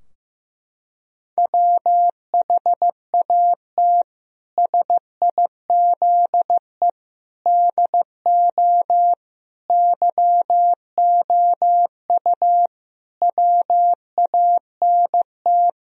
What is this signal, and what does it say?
Morse code, heard as a keyed tone, about 16 wpm, 700 Hz. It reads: WHAT SIZE DO YOU WANT
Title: Sentences from Top 1000 Words - Review Part 2 - 15wpm